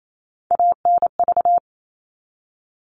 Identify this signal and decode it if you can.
Morse code, heard as a keyed tone, about 28 wpm, 700 Hz. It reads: ANV
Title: All Cut Numbers - 3-Digit 28wpm